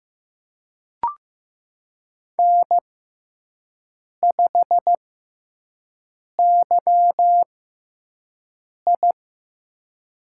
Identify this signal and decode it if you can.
Morse code, heard as a keyed tone, about 15 wpm, spaced as for 5 wpm, 700 Hz. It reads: N5YI
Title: USA Call Signs - 4 Characters - 6x Character Spacing - 15wpm